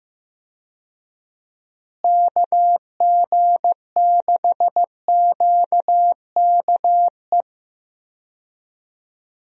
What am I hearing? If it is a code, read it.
Morse code, heard as a keyed tone, about 15 wpm, 700 Hz. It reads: KG6QKE
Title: USA Call Signs - 6 Characters - Repeated 2x - 15wpm